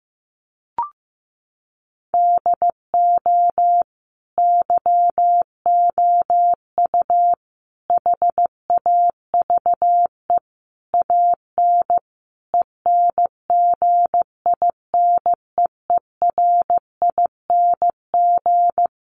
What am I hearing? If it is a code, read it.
Morse code, heard as a keyed tone, about 15 wpm, 700 Hz. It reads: DO YOU HAVE AN ENGINEERING
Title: Sentences from Top 1500 Words 15wpm